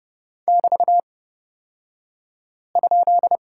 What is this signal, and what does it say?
Morse code, heard as a keyed tone, about 30 wpm, spaced as for 5 wpm, 700 Hz. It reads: <BT>?
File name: Lesson 37d - Letter x - Two Characters 30wpm